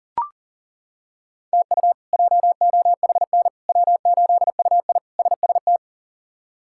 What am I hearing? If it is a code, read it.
Morse code, heard as a keyed tone, about 40 wpm, 700 Hz. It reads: TU JOHN W8UI SST
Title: K1USN SST QSO Practice - 40wpm